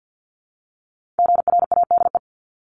Morse code, heard as a keyed tone, about 50 wpm, 700 Hz. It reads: GRADE
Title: Multi-letter Phonemes 50wpm